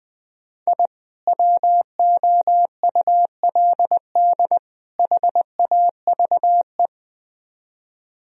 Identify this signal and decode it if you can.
Morse code, heard as a keyed tone, about 22 wpm, 700 Hz. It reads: I WOULD HAVE
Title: Speed-Racing - Sets of 3 Words 20wpm